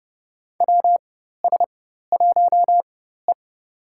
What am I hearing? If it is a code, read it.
Morse code, heard as a keyed tone, about 30 wpm, spaced as for 14 wpm, 700 Hz. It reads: WS1E
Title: USA Call Signs - 4 Characters - 4x Character Spacing - 30wpm